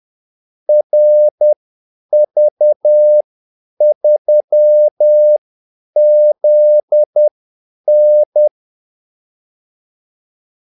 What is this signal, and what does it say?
Morse code, heard as a keyed tone, about 10 wpm, spaced as for 8 wpm, 600 Hz. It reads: RV3ZN
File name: Callsigns-10-8wpm